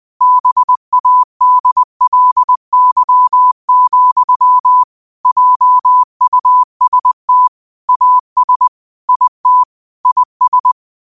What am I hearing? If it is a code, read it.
Morse code, heard as a keyed tone, about 20 wpm, 1000 Hz. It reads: BADLY, JUST AS IT IS
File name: qotdfiles / 20wpm